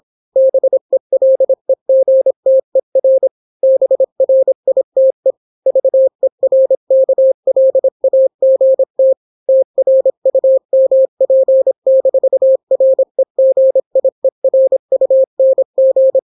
Morse code, heard as a keyed tone, about 26 wpm, 530 Hz. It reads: BELEGTER BRITE VERKLAGT TRUMP-REGIERUNG